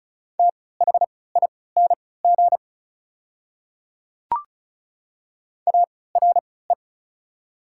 Morse code, heard as a keyed tone, about 34 wpm, spaced as for 20 wpm, 700 Hz. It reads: THING ARE
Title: 3x Character Spacing - Top 200 Words - 35wpm